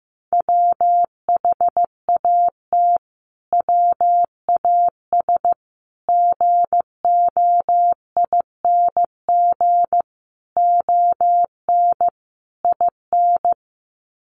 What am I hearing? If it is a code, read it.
Morse code, heard as a keyed tone, about 15 wpm, 700 Hz. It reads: WHAT WAS GOING ON IN
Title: Sets of 5 Words Encore 15wpm